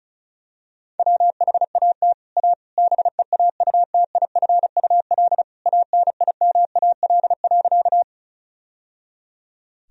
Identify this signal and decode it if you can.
Morse code, heard as a keyed tone, about 36 wpm, 700 Hz. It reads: WHAT A BEAUTIFUL ANIMAL.
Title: Sentences from Top 1000 Words - Review Part 1 - 35wpm